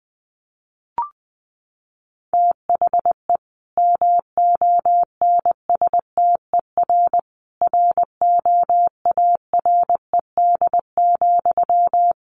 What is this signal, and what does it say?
Morse code, heard as a keyed tone, about 20 wpm, 700 Hz. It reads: THE MONSTER ROARED,
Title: A Sound Of Thunder 20wpm